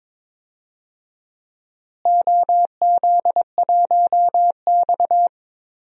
Morse code, heard as a keyed tone, about 22 wpm, 700 Hz. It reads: OZ1X
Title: DXCC-100 Call Signs - 4 Characters - Repeated 2x - 22wpm